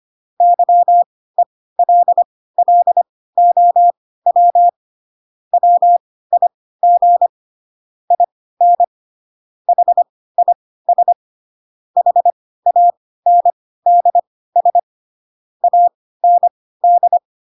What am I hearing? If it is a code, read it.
Morse code, heard as a keyed tone, about 25 wpm, spaced as for 16 wpm, 700 Hz. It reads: YELLOW WIG IN HIS HANDS AND
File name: Pinocchio / Pinocchio ch2 10 25 ik1hge